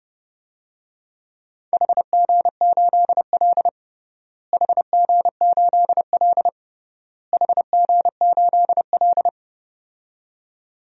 Morse code, heard as a keyed tone, about 32 wpm, 700 Hz. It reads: HG8L HG8L HG8L
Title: DXCC-100 Call Signs - 4 Characters - Repeated 3x - 30wpm